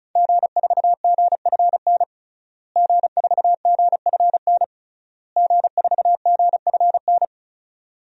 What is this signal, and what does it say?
Morse code, heard as a keyed tone, about 36 wpm, 700 Hz. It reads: G4GFN G4GFN G4GFN
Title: DXCC-100 Call Signs - 5 Characters - Repeated 3x - 35wpm